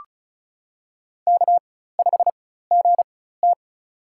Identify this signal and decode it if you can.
Morse code, heard as a keyed tone, about 36 wpm, spaced as for 16 wpm, 700 Hz. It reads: K5GT
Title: USA Call Signs - 4 Characters - 4x Character Spacing - 35wpm